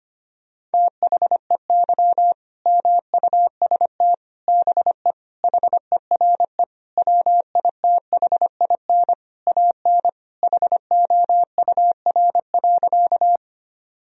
Morse code, heard as a keyed tone, about 25 wpm, 700 Hz. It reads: THEY MUST BE HERE WITHIN AN HOUR.
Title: Sentences from Top 1000 Words - Review Part 4 - 25wpm